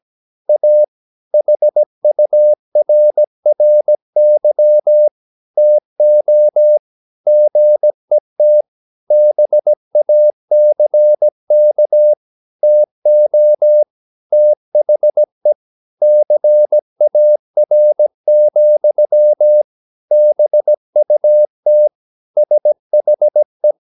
Morse code, heard as a keyed tone, about 17 wpm, 600 Hz. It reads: A HURRY TO GET BACK TO THE CAR, BUT SHE
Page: Romanen Never af Ken Follet på engelsk.